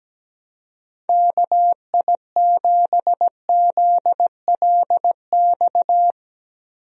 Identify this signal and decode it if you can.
Morse code, heard as a keyed tone, about 17 wpm, 700 Hz. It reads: KI7ZLX
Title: USA Call Signs - 6 Characters - Repeated 2x - 17wpm